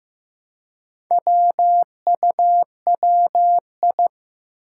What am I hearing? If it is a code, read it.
Morse code, heard as a keyed tone, about 15 wpm, 700 Hz. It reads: WUWI
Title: Four Challenging Letter-Number Combinations 15wpm